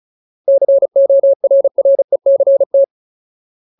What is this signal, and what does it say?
Morse code, heard as a keyed tone, about 34 wpm, 550 Hz. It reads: CORRECT
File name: Ultimate Word List / 35WPM Ultimate